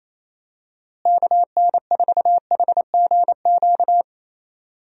Morse code, heard as a keyed tone, about 28 wpm, 700 Hz. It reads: KN4HGQ
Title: USA Call Signs - 6 Characters - Repeated 3x - 28wpm